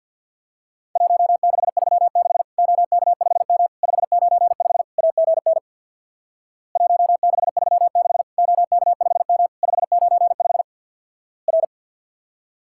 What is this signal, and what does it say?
Morse code, heard as a keyed tone, about 51 wpm, 700 Hz. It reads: 1636 OK5M 595 AGN 1636 OK5M 595 R
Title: Worked All Europe DX Contest - QTC Exchange with Repeats - 50wpm